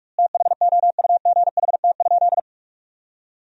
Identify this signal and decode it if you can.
Morse code, heard as a keyed tone, about 46 wpm, 700 Hz. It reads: THOUGHT?
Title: 2x Word-Spacing - Sentences from Top 300 Words - 45wpm